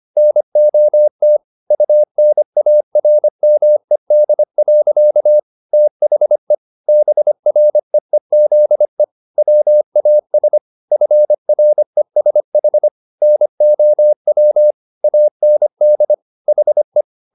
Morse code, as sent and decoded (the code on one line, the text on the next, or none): -. --- - / ..- -. .- .-. -- . -.. .-.-.- / - .... / -... .-. . --.. / .-- .- ... / ..-. .-. ... .... / -. --- .-- / .- -. -.. / .... .
NOT UNARMED. THE BREEZE WAS FRESH NOW AND HE